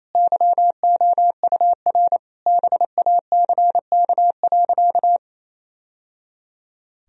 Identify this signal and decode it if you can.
Morse code, heard as a keyed tone, about 28 wpm, 700 Hz. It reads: YOUR BACK.
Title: Sentences from Top 1000 Words - Review Part 1 - 28wpm